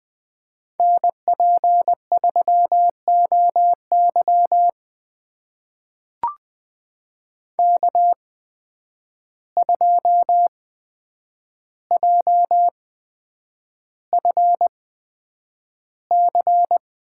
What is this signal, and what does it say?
Morse code, heard as a keyed tone, about 20 wpm, spaced as for 9 wpm, 700 Hz. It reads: NP3OY K2JFC
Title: USA Call Signs - 5 Characters - 8x Character Spacing - 20wpm